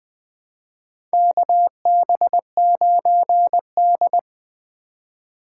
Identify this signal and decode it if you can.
Morse code, heard as a keyed tone, about 20 wpm, 700 Hz. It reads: KB9D
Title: USA Call Signs - 4 Characters - 2x Character Spacing - 20wpm